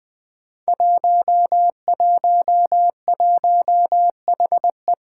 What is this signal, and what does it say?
Morse code, heard as a keyed tone, about 20 wpm, 700 Hz. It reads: 111HE
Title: Five Challenging Letter-Number Combinations 20wpm